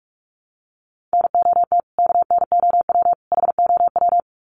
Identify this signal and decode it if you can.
Morse code, heard as a keyed tone, about 44 wpm, 700 Hz. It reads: NOT KNOW HOW
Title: Sets of 3 Words 45wpm